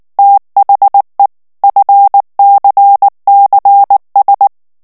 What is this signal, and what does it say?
Morse code, heard as a keyed tone, about 19 wpm, 800 Hz. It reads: THE FCCS
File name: ANGLAIS texte clair 3 vitesse 18 mots minute